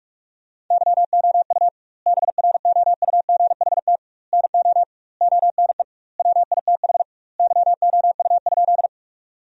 Morse code, heard as a keyed tone, about 41 wpm, 700 Hz. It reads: YOU BROUGHT NO ONE WITH YOU?
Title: 2x Word-Spacing - Sentences from Top 500 Words - 45wpm